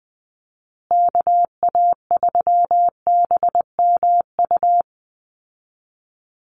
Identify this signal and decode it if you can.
Morse code, heard as a keyed tone, about 20 wpm, 700 Hz. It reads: KA3BMU
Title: Call Signs Repeated 2x - 20wpm